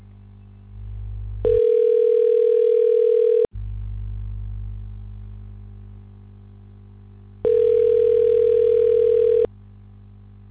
Ringback tone (North America), 440 + 480 Hz.